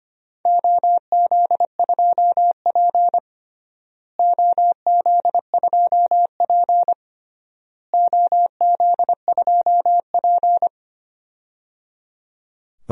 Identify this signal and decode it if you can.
Morse code, heard as a keyed tone, about 25 wpm, 700 Hz. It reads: OZ2P OZ2P OZ2P
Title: DXCC-100 Call Signs - 4 Characters - Repeated 3x - 25wpm